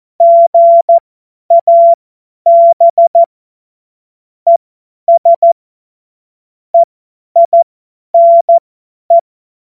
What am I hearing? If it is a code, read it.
Morse code, heard as a keyed tone, about 14 wpm, spaced as for 10 wpm, 680 Hz. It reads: GAB ES EINE
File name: Morsekurs / Witze / Witz 05 März 2024 50 F70